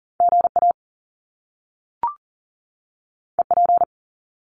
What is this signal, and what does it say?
Morse code, heard as a keyed tone, about 39 wpm, 700 Hz. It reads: GA EP
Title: Top Two Letter Combinations in Words 40wpm